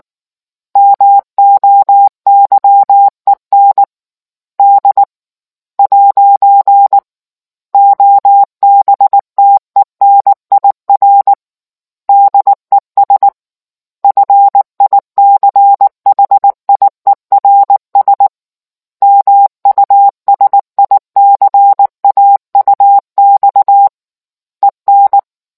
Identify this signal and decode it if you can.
Morse code, heard as a keyed tone, about 19 wpm, 800 Hz. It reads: MOYEN D'OBTENIR DES FICHIERS MUSICAUX EN